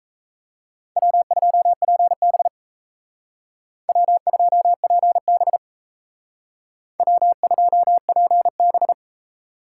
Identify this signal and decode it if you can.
Morse code, heard as a keyed tone, about 36 wpm, 700 Hz. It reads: W2PB W2PB W2PB